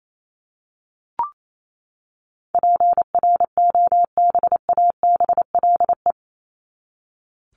Morse code, heard as a keyed tone, about 28 wpm, 700 Hz. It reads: PROBABLE
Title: LY Suffix - Top 100 Words with Base Words 28wpm